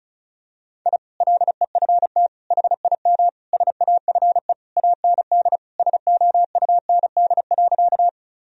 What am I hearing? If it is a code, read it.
Morse code, heard as a keyed tone, about 36 wpm, 700 Hz. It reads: I LEFT HIM SAFE AND SOUND.
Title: Sentences from Top 1000 Words - Review Part 1 - 35wpm